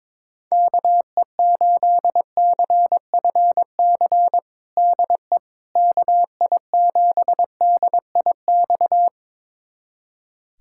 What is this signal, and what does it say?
Morse code, heard as a keyed tone, about 22 wpm, 700 Hz. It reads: KE8CFC DE KI7DIX
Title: USA Callsign DE Callsign - 6 Characters - 22wpm